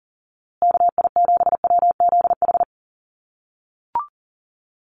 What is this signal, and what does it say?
Morse code, heard as a keyed tone, about 40 wpm, 700 Hz. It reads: KI7WZH
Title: USA Call Signs Repeated 2x 40wpm